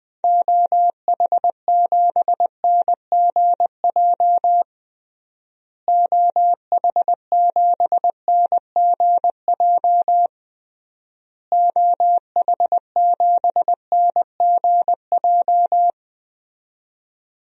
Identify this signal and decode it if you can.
Morse code, heard as a keyed tone, about 20 wpm, 700 Hz. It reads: OH7NGJ OH7NGJ OH7NGJ
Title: DXCC-100 Call Signs - 6 Characters - Repeated 3x - 20wpm